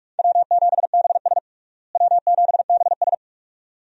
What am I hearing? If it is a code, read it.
Morse code, heard as a keyed tone, about 44 wpm, 700 Hz. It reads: W7BS W7BS
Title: USA Call Signs - 4 Characters - Repeated 2x - 45wpm